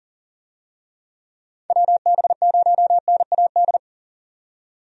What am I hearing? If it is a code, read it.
Morse code, heard as a keyed tone, about 40 wpm, 700 Hz. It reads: WB0NAD
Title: USA Call Signs - 6 Characters - 40wpm